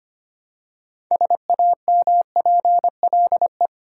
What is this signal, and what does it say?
Morse code, heard as a keyed tone, about 25 wpm, 700 Hz. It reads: SAMPLE
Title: Six Letter Words - 25wpm